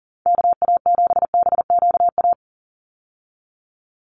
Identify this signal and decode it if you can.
Morse code, heard as a keyed tone, about 40 wpm, 700 Hz. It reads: KA7BQA
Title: USA Call Signs 40wpm